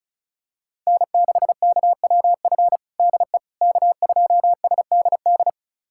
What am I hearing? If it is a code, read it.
Morse code, heard as a keyed tone, about 36 wpm, 700 Hz. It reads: N6KWF DE K2SDD
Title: USA Callsign DE Callsign - 5 Characters - 35wpm